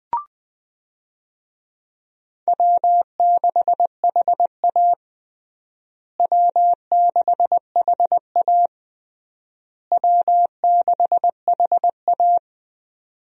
Morse code, heard as a keyed tone, about 20 wpm, 700 Hz. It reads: W6HA W6HA W6HA
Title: USA Call Signs - 4 Characters - Repeated 3x - 20wpm